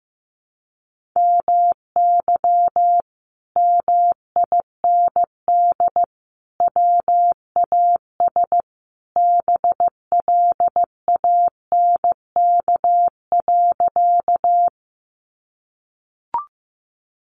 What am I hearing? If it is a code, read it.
Morse code, heard as a keyed tone, about 15 wpm, 700 Hz. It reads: MY MIND WAS BLANK.
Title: Sentences from Top 1500 Words 15wpm